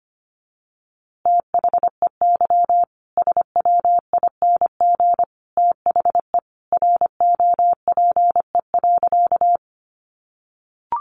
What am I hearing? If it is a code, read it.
Morse code, heard as a keyed tone, about 25 wpm, 700 Hz. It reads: THEY SWING THE ROPE.
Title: Easy Phrases 25wpm